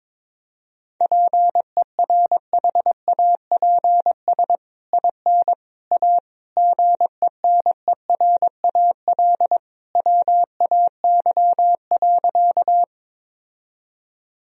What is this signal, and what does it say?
Morse code, heard as a keyed tone, about 22 wpm, 700 Hz. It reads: PERHAPS IN A GENERAL WAY.
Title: Sentences from Top 1000 Words - Review Part 3 - 22wpm